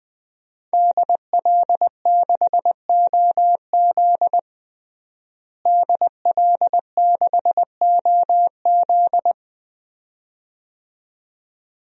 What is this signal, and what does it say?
Morse code, heard as a keyed tone, about 20 wpm, 700 Hz. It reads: DL6OZ DL6OZ
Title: DXCC-100 Call Signs - 5 Characters - Repeated 2x - 20wpm